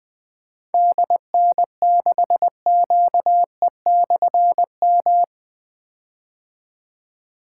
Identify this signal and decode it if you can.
Morse code, heard as a keyed tone, about 20 wpm, 700 Hz. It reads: DN6QE/M
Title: Difficult Call Signs 20wpm